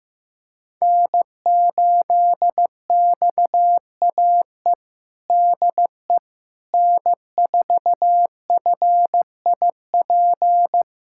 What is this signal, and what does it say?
Morse code, heard as a keyed tone, about 16 wpm, 700 Hz. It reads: N8XAE DE N4FIP
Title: USA Callsign DE Callsign - 5 Characters - 15wpm